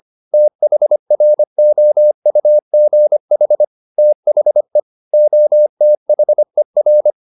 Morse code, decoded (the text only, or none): THROUGH THE OTHER